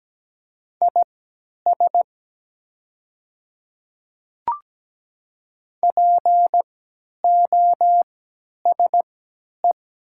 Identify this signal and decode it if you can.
Morse code, heard as a keyed tone, about 17 wpm, spaced as for 10 wpm, 690 Hz. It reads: IS POSE
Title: 3x Character Spacing - Top 500 Words - 17wpm